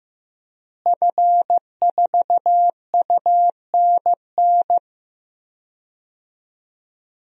Morse code, heard as a keyed tone, about 15 wpm, 700 Hz. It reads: F4UNN
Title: DXCC-100 Call Signs - 5 Characters - Repeated 2x - 15wpm